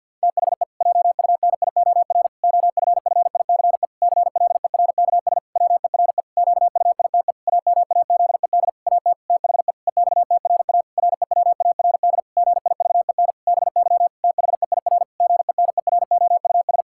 Morse code, heard as a keyed tone, about 50 wpm, 700 Hz. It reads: THE JUNIOR OFFICE CLERKS WERE QUITE AMAZED AT THE EXTRA REWARD GIVEN BY THEIR GENEROUS
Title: Pangram Challenge v2 - 50wpm